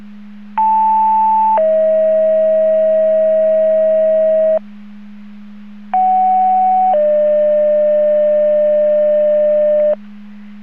Two-tone page, heard as a one-second tone, then a three-second tone, about 880 then 640 Hz, about 750 then 600 Hz. It is heard from 0.6 to 9.9 s.
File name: Portage / 00187 09-13 Portage 154.385 Paging